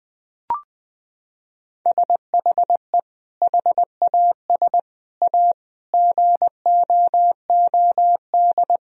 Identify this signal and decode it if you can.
Morse code, heard as a keyed tone, about 21 wpm, 700 Hz. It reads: SHE HAS A GOOD
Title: Sentences from Top 1000 Words - Review Part 2 - 20wpm